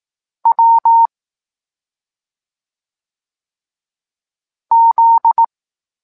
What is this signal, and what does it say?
Morse code, heard as a keyed tone, about 18 wpm, 920 Hz. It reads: W Z
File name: Tekst-23 4